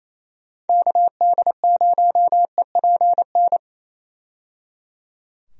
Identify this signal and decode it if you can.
Morse code, heard as a keyed tone, about 28 wpm, 700 Hz. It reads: KD0EPN
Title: USA Call Signs - 6 Characters - 28wpm